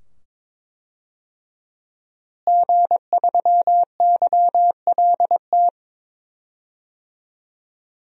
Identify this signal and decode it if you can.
Morse code, heard as a keyed tone, about 22 wpm, 700 Hz. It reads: G3YLT
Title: DXCC-100 Call Signs - 5 Characters - Repeated 2x - 22wpm